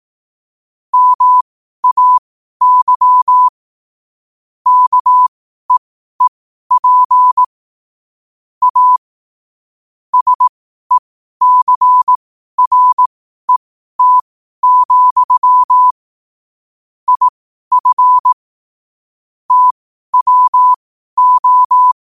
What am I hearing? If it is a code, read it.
Morse code, heard as a keyed tone, about 18 wpm, spaced as for 12 wpm, 1000 Hz. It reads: MAY KEEP A SECRET, IF TWO
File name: qotdfiles / 12wpm